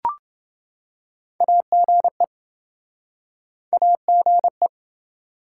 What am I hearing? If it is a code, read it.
Morse code, heard as a keyed tone, about 27 wpm, 700 Hz. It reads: AGE AGE